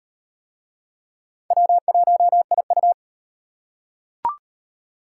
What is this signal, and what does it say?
Morse code, heard as a keyed tone, about 38 wpm, 700 Hz. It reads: W1IU